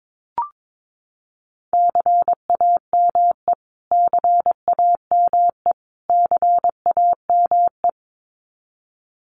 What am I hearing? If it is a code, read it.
Morse code, heard as a keyed tone, about 22 wpm, 700 Hz. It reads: CAME CAME CAME
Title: Top 500 Words Repeated 22wpm